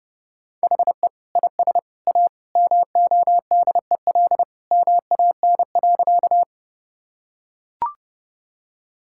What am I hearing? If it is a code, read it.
Morse code, heard as a keyed tone, about 31 wpm, 700 Hz. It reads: HE IS A MODEL MAN.
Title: Sentences from Top 1000 Words - Review Part 5 - 30wpm